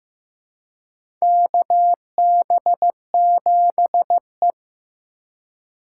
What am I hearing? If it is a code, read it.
Morse code, heard as a keyed tone, about 15 wpm, 700 Hz. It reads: KB7E
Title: USA Call Signs - 4 Characters - 4x Character Spacing - 15wpm